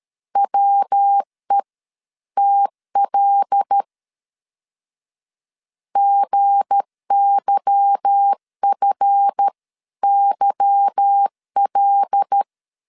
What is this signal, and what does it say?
Morse code, heard as a keyed tone, about 13 wpm, 800 Hz. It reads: WE TL GYFYL
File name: LEZIONI / 8 YJ45 / 8 Y 12wpm 2228